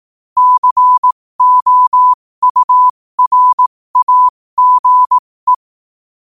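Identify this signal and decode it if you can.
Morse code, heard as a keyed tone, about 18 wpm, spaced as for 15 wpm, 1000 Hz. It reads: COURAGE
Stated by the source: Morse code quote of the day sent at 15 WPM.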